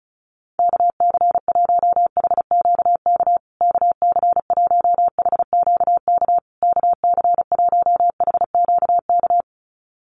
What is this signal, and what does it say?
Morse code, heard as a keyed tone, about 34 wpm, 700 Hz. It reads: KC1HQK KC1HQK KC1HQK
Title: USA Call Signs Repeated 3x 35wpm